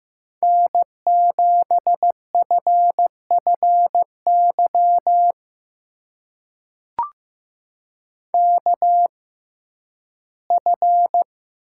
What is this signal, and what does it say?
Morse code, heard as a keyed tone, about 15 wpm, spaced as for 11 wpm, 700 Hz. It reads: N7FFY KF
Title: USA Call Signs - 5 Characters - 6x Character Spacing - 15wpm